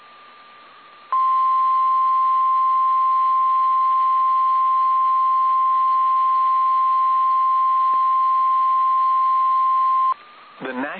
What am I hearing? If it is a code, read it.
Emergency Alert System (SAME) tones (the 1050 Hz attention tone), with speech. It is heard from 1.1 to 10.2 s.